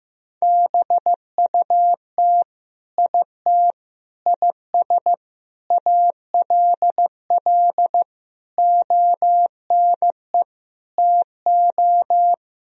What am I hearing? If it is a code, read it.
Morse code, heard as a keyed tone, about 15 wpm, 700 Hz. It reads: BUT IT IS ALL ONE TO
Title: Sentences from Top 1000 Words - Review Part 1 - 15wpm